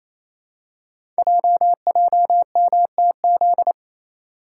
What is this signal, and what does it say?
Morse code, heard as a keyed tone, about 28 wpm, 700 Hz. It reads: JJMTZ
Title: Code Groups - 5 Letters - 28wpm